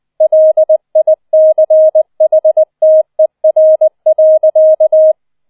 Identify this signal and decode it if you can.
Morse code, heard as a keyed tone, about 20 wpm, 630 Hz. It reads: LICHTER.
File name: 007 Letzter Streich 20wpm